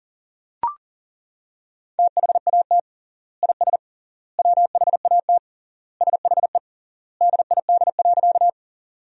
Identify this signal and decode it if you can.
Morse code, heard as a keyed tone, about 40 wpm, 700 Hz. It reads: THAT IS WHAT SHE DID.
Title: 3x Word-Spacing - Sentences from Top 100 Words - 40wpm